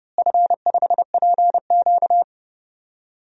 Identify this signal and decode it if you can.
Morse code, heard as a keyed tone, about 28 wpm, 700 Hz. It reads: F5PQ
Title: DXCC-100 Call Signs - 4 Characters - Repeated 3x - 30wpm